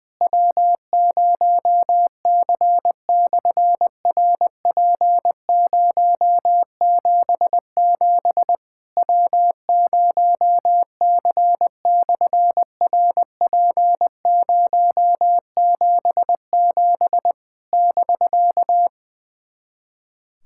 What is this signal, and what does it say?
Morse code, heard as a keyed tone, about 20 wpm, 700 Hz. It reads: W0C/RP077 W0C/RP077 <BK>
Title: SOTA QSO Practice 20wpm